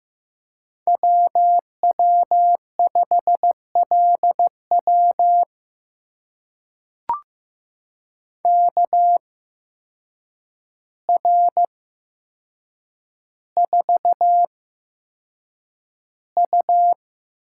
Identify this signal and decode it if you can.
Morse code, heard as a keyed tone, about 15 wpm, spaced as for 6 wpm, 700 Hz. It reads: W W5LW KR4U
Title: USA Call Signs - 8x Character Spacing - 15wpm